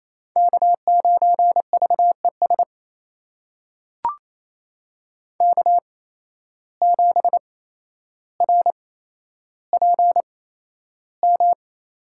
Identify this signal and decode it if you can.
Morse code, heard as a keyed tone, about 28 wpm, spaced as for 12 wpm, 700 Hz. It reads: K9VES K7RPM
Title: USA Call Signs - 5 Characters - 8x Character Spacing - 28wpm